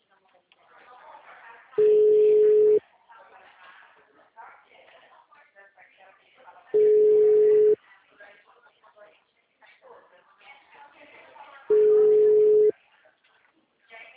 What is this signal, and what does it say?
Ringback tone (Europe, 425 Hz), with speech.